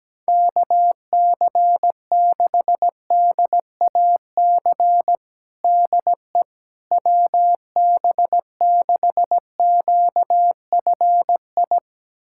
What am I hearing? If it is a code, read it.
Morse code, heard as a keyed tone, about 17 wpm, 700 Hz. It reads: KC6DAC DE WB6QFI
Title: USA Callsign DE Callsign - 6 Characters - 17wpm